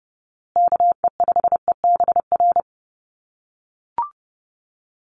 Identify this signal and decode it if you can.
Morse code, heard as a keyed tone, about 30 wpm, 700 Hz. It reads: KE5EBR
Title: USA Call Signs Repeated 2x 30wpm